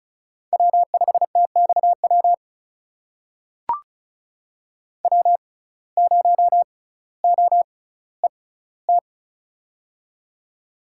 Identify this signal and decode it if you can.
Morse code, heard as a keyed tone, about 36 wpm, spaced as for 18 wpm, 700 Hz. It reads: W5TXW W0OET
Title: USA Call Signs - 5 Characters - 6x Character Spacing - 35wpm